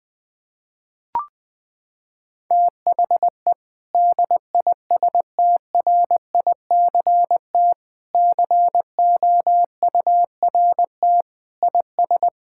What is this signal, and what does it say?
Morse code, heard as a keyed tone, about 20 wpm, 700 Hz. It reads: THE DISTRICT COURT IS
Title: Sentences from Top 1600 Words 20wpm